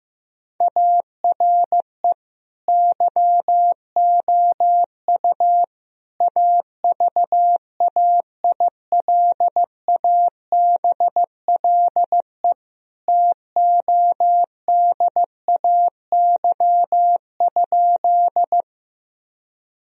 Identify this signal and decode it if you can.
Morse code, heard as a keyed tone, about 14 wpm, 700 Hz. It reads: ARE YOU AVAILABLE TODAY?
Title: Sentences from Top 1000 Words - Review Part 3 - 15wpm